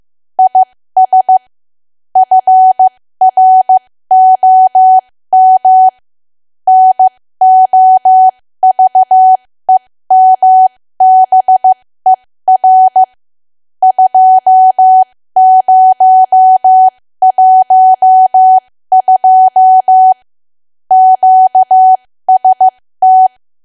Morse code, heard as a keyed tone, about 15 wpm, spaced as for 13 wpm, 750 Hz. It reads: IS FROM NOVEMBER 2012 QST
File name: cwa / W1AW-Level2 / 13WPM